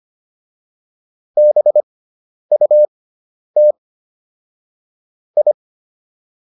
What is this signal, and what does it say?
Morse code, heard as a keyed tone, about 25 wpm, spaced as for 10 wpm, 600 Hz. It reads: BUT I